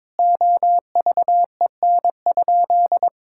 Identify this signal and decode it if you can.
Morse code, heard as a keyed tone, about 21 wpm, 700 Hz. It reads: OVEN?
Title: Sentences from Top 1000 Words - Review Part 6 - 22wpm